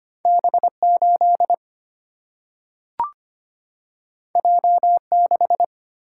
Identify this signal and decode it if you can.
Morse code, heard as a keyed tone, about 25 wpm, 700 Hz. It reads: B8 J6
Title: Single-Letter Single-Number 25wpm